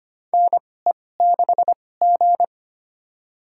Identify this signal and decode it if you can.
Morse code, heard as a keyed tone, about 25 wpm, spaced as for 18 wpm, 700 Hz. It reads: NE6G
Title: USA Call Signs - 4 Characters - 2x Character Spacing - 25wpm